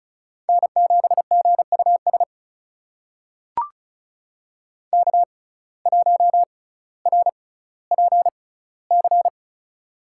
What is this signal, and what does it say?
Morse code, heard as a keyed tone, about 36 wpm, spaced as for 18 wpm, 700 Hz. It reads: N7GUS K1RPC
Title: USA Call Signs - 5 Characters - 6x Character Spacing - 35wpm